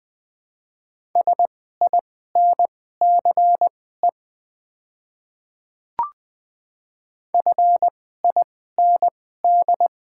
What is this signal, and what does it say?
Morse code, heard as a keyed tone, about 20 wpm, spaced as for 14 wpm, 690 Hz. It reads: SINCE FIND
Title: 2x Character Spacing - Top 500 Words - 20wpm